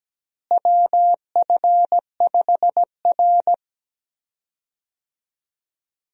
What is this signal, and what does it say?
Morse code, heard as a keyed tone, about 17 wpm, 700 Hz. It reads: WF5R
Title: USA Call Signs - 4 Characters - Repeated 3x - 17wpm